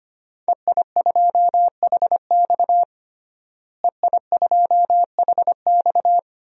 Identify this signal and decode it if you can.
Morse code, heard as a keyed tone, about 25 wpm, 700 Hz. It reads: EI2HX EI2HX
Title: DXCC-100 Call Signs - 5 Characters - Repeated 2x - 25wpm